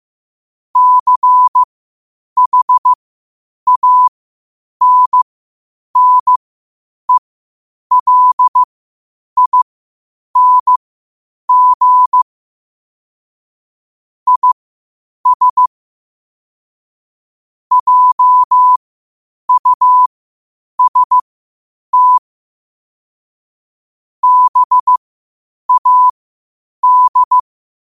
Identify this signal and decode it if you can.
Morse code, heard as a keyed tone, about 15 wpm, spaced as for 8 wpm, 1000 Hz. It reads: CHANNELING IS JUST BAD
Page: New quotes every day in morse code at 8 Words per minute.